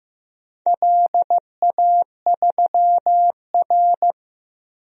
Morse code, heard as a keyed tone, about 15 wpm, 700 Hz. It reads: LA3R
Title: DXCC-100 Call Signs - 4 Characters - Repeated 3x - 15wpm